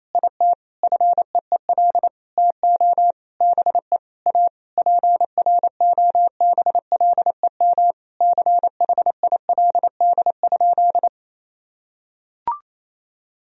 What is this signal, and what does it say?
Morse code, heard as a keyed tone, about 28 wpm, 700 Hz. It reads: IT FEEL TO BE A PROBLEM CHILD?
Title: Sentences from Top 1000 Words - Review Part 2 - 28wpm